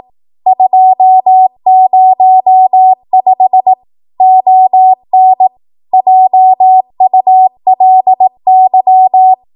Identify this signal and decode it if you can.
Morse code, heard as a keyed tone, about 18 wpm, 750 Hz. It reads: 205 ON JULY
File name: Archive / 18 WPM / 18WPM